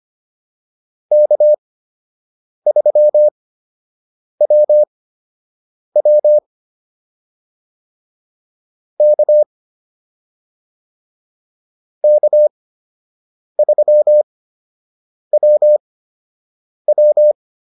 Morse code, heard as a keyed tone, about 25 wpm, spaced as for 7 wpm, 600 Hz. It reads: K3WW K K3WW